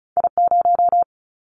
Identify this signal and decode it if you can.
Morse code, heard as a keyed tone, about 34 wpm, 700 Hz. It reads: I0
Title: Callsign Prefixes for Top 25 DXCC - 35wpm